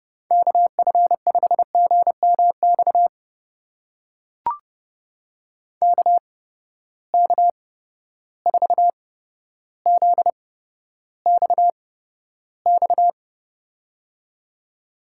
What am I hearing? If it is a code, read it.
Morse code, heard as a keyed tone, about 32 wpm, spaced as for 13 wpm, 700 Hz. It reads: KF5GMX KK4ZXX
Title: USA Call Signs - 6 Characters - 8x Character Spacing - 30wpm